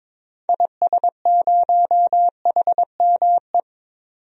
Morse code, heard as a keyed tone, about 21 wpm, 700 Hz. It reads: IS0HME
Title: DXCC-100 Call Signs - 6 Characters - 22wpm